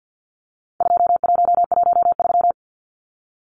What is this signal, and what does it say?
Morse code, heard as a keyed tone, about 50 wpm, 700 Hz. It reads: 2113
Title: Numbers - 4-Digit 50wpm